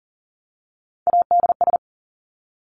Morse code, heard as a keyed tone, about 38 wpm, 700 Hz. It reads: ADS
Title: Abbreviations 40wpm